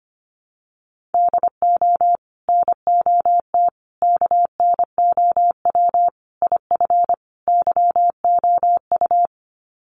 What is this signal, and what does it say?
Morse code, heard as a keyed tone, about 25 wpm, 700 Hz. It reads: DO NOT KNOW IF YOU
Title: Sets of 5 Words 25wpm